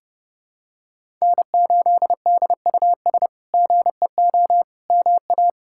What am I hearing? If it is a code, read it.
Morse code, heard as a keyed tone, about 30 wpm, 700 Hz. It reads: N8DUS GEO MA
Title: K1USN SST QSO Practice - 30wpm